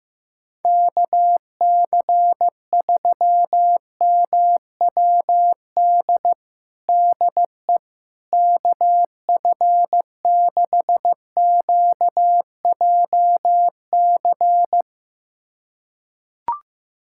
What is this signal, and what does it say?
Morse code, heard as a keyed tone, about 16 wpm, 700 Hz. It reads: KC3MWD DE KF6QJC
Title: USA Callsign DE Callsign 15wpm